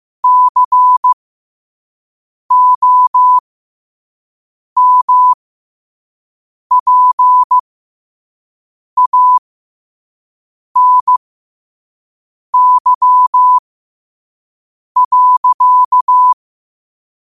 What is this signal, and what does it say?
Morse code, heard as a keyed tone, about 15 wpm, spaced as for 5 wpm, 1000 Hz. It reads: COMPANY.
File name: qotdfiles / 5wpm